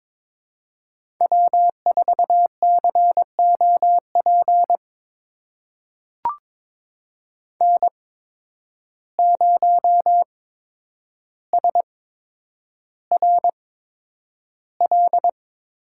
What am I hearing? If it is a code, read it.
Morse code, heard as a keyed tone, about 22 wpm, spaced as for 9 wpm, 700 Hz. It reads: W4COP N0SRL
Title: USA Call Signs - 5 Characters - 8x Character Spacing - 22wpm